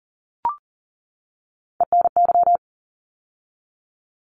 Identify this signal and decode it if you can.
Morse code, heard as a keyed tone, about 40 wpm, 700 Hz. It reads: ENY
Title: ARRL Sections 40wpm